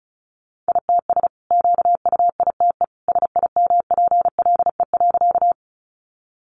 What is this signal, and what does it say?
Morse code, heard as a keyed tone, about 34 wpm, 700 Hz. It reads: ITS QUITE SIMPLE.
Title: Sentences from Top 700 Words 35wpm